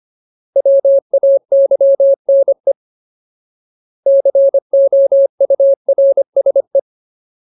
Morse code, heard as a keyed tone, about 25 wpm, 550 Hz. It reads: WAYNE COURSE
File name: Ultimate Word List / 25WPM Ultimate